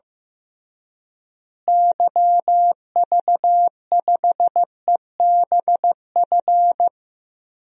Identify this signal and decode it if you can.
Morse code, heard as a keyed tone, about 15 wpm, 700 Hz. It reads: YV5EBF
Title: DXCC-100 Call Signs - 6 Characters - Repeated 2x - 15wpm